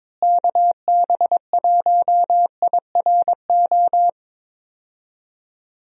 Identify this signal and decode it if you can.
Morse code, heard as a keyed tone, about 22 wpm, 700 Hz. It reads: KB1IRO
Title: USA Call Signs - 6 Characters - Repeated 3x - 22wpm